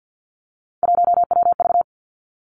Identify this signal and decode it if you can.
Morse code, heard as a keyed tone, about 50 wpm, 700 Hz. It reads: 1WV
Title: Three Challenging Letter-Number Combinations 50wpm